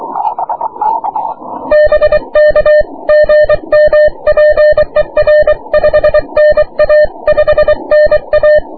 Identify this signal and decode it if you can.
Morse code, heard as a keyed tone, about 23 wpm, 600 Hz. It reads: BK GM PER 5NA 5NA
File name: wp / 6m aurora cw 22jul 2009